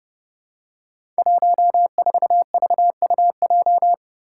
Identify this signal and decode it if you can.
Morse code, heard as a keyed tone, about 30 wpm, 700 Hz. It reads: 14VUJ
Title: Five Challenging Letter-Number Combinations 30wpm